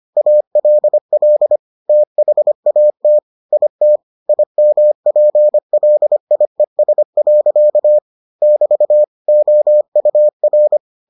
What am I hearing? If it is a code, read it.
Morse code, heard as a keyed tone, about 25 wpm, 600 Hz. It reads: ALL THAT IT IMPLIES. <BT> OUR